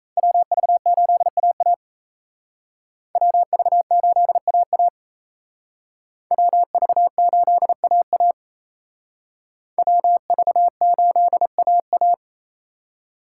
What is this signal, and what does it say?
Morse code, heard as a keyed tone, about 35 wpm, 700 Hz. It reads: WV8AA WV8AA WV8AA WV8AA